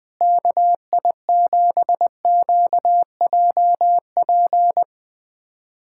Morse code, heard as a keyed tone, about 20 wpm, 700 Hz. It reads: KI7QJP
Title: USA Callsign DE Callsign - 6 Characters - 20wpm